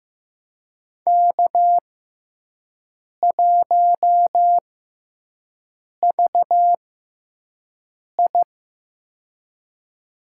Morse code, heard as a keyed tone, about 15 wpm, spaced as for 5 wpm, 700 Hz. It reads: K1VI
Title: USA Call Signs - 4 Characters - 6x Character Spacing - 15wpm